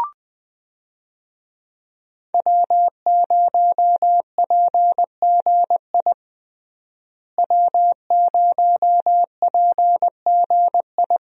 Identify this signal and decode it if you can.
Morse code, heard as a keyed tone, about 21 wpm, 700 Hz. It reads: W0PGI W0PGI
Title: USA Call Signs - 5 Characters - Repeated 3x - 20wpm